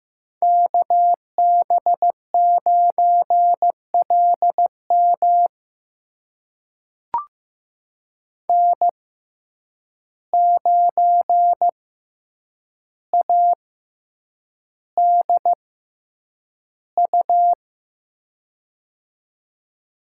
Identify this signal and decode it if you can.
Morse code, heard as a keyed tone, about 15 wpm, spaced as for 8 wpm, 700 Hz. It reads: KB9LM N9ADU
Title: USA Call Signs - 5 Characters - 6x Character Spacing - 15wpm